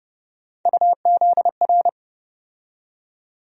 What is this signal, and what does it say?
Morse code, heard as a keyed tone, about 30 wpm, 700 Hz. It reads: UZR
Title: USA Call Sign Postfixes - 3 Letters 30wpm